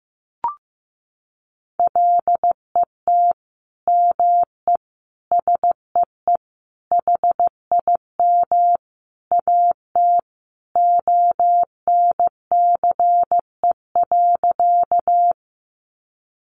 Morse code, heard as a keyed tone, about 15 wpm, 700 Hz. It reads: LET ME SEE HIM AT ONCE.
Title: Sentences from Top 300 Words 15wpm